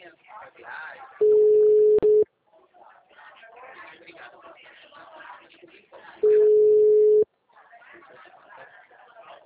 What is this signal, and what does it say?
Ringback tone (Europe, 425 Hz), with speech.